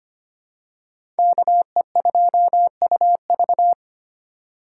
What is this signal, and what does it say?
Morse code, heard as a keyed tone, about 25 wpm, 700 Hz. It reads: KE2UV
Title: USA Call Signs - 5 Characters - 4x Character Spacing - 25wpm